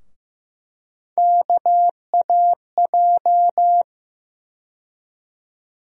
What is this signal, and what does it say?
Morse code, heard as a keyed tone, about 15 wpm, 700 Hz. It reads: KAJ
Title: USA Call Sign Postfixes - 3 Letters 15wpm